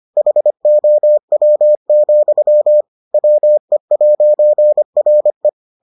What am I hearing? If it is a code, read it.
Morse code, heard as a keyed tone, about 25 wpm, 600 Hz. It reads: HOW, WE'RE